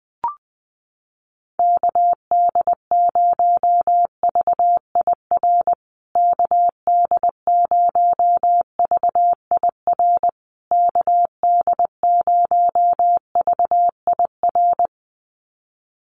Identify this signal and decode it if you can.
Morse code, heard as a keyed tone, about 19 wpm, 700 Hz. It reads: KD0VIR KD0VIR KD0VIR
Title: USA Call Signs Repeated 3x 20wpm